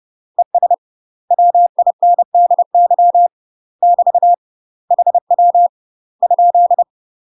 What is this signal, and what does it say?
Morse code, heard as a keyed tone, about 30 wpm, 700 Hz. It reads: ES WINDY <BT> HW ?
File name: UK MASTER QSOs TRAINING/30WPM